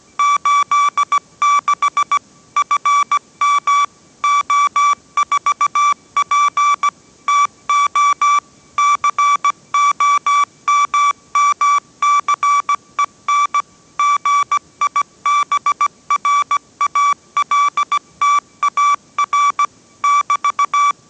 Morse code, heard as a keyed tone, about 17 wpm, 1160 Hz. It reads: 86 FM O4P TO COMMCEN GIBRALTAR <BT>